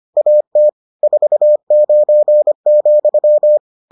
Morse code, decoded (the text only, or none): AT 49,